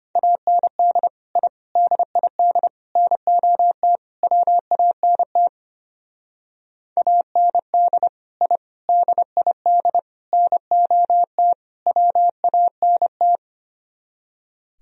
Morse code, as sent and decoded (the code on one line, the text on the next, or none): .- -. -.. / .. / -.. .. -.. / -. --- - / .-- .- -. - / .- -. -.. / .. / -.. .. -.. / -. --- - / .-- .- -. -
AND I DID NOT WANT AND I DID NOT WANT